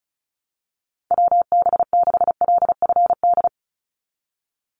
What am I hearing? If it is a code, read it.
Morse code, heard as a keyed tone, about 34 wpm, 700 Hz. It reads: WB6LFD
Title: USA Call Signs 35wpm